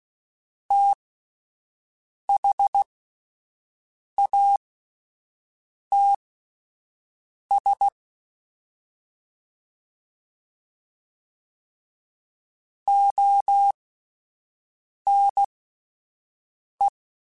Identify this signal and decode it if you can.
Morse code, heard as a keyed tone, about 16 wpm, spaced as for 5 wpm, 800 Hz. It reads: THATS ONE
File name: (spaced) / QSOs Spaced-mp3s / WD0FBT 1 @10 sp